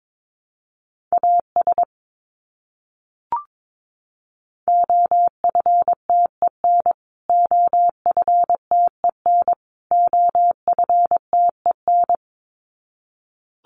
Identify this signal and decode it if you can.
Morse code, heard as a keyed tone, about 22 wpm, 700 Hz. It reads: AS OFTEN OFTEN OFTEN
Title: Top 500 Words Repeated 22wpm